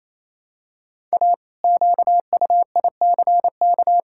Morse code, heard as a keyed tone, about 28 wpm, 700 Hz. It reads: A QUICK
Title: Pangram Challenge v2 - 28wpm